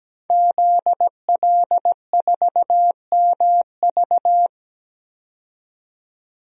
Morse code, heard as a keyed tone, about 17 wpm, 700 Hz. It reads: ZL4MV
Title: DXCC-100 Call Signs - 5 Characters - 17wpm